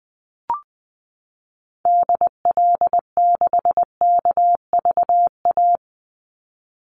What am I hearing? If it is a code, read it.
Morse code, heard as a keyed tone, about 20 wpm, 700 Hz. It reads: DL6KVA
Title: Call Signs Repeated 2x - 20wpm